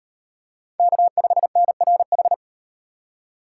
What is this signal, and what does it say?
Morse code, heard as a keyed tone, about 38 wpm, 700 Hz. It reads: K5NRH